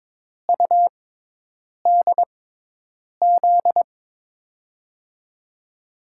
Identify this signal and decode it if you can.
Morse code, heard as a keyed tone, about 22 wpm, spaced as for 8 wpm, 700 Hz. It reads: UDZ
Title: USA Call Signs - 6 Characters - 6x Character Spacing - 22wpm